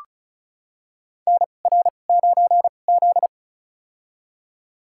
Morse code, heard as a keyed tone, about 34 wpm, spaced as for 25 wpm, 700 Hz. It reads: NR9Z
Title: USA Call Signs - 4 Characters - 2x Character Spacing - 35wpm